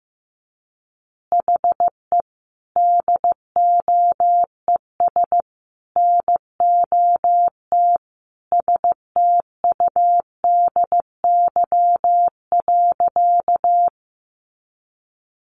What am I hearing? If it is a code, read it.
Morse code, heard as a keyed tone, about 15 wpm, 700 Hz. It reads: HE DOES NOT STUDY.
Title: Additional Easy Phrases 15wpm